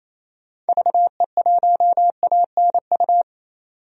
Morse code, heard as a keyed tone, about 28 wpm, 700 Hz. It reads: VE1ANU
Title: DXCC-100 Call Signs - 6 Characters - 28wpm